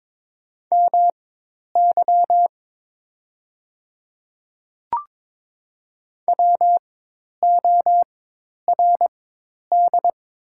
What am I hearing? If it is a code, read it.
Morse code, heard as a keyed tone, about 22 wpm, spaced as for 10 wpm, 690 Hz. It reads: MY WORD
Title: 4x Character Spacing - Top 100 Words - 22wpm